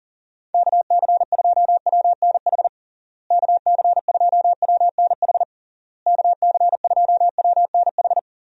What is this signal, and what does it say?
Morse code, heard as a keyed tone, about 40 wpm, 700 Hz. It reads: KC2WNH KC2WNH KC2WNH
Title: USA Call Signs - 6 Characters - Repeated 3x - 40wpm